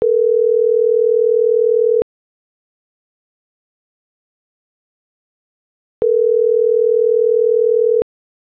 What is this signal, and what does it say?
Ringback tone (North America), 440 + 480 Hz.